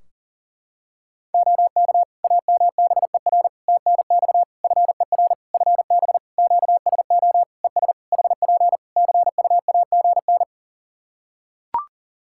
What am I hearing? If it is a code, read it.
Morse code, heard as a keyed tone, about 40 wpm, 700 Hz. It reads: OK AMBER TNX FER FB QSO ES HP CUAGN
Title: Practice QSO - Exchange 4 - Ending - 40wpm